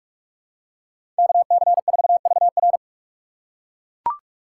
Morse code, heard as a keyed tone, about 42 wpm, 700 Hz. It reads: KC4VR
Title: USA Call Signs - 6x Character Spacing - 45wpm